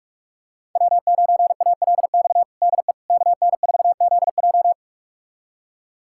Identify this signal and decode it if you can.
Morse code, heard as a keyed tone, about 46 wpm, 700 Hz. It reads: W9ALX DE KN4ZJ
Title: USA Callsign DE Callsign - 5 Characters - 45wpm